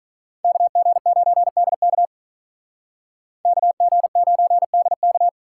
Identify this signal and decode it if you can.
Morse code, heard as a keyed tone, about 44 wpm, 700 Hz. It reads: KG9DK KG9DK